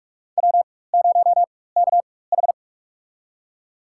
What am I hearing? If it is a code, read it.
Morse code, heard as a keyed tone, about 43 wpm, spaced as for 21 wpm, 700 Hz. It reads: W0KH
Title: USA Call Signs - 4 Characters - 4x Character Spacing - 45wpm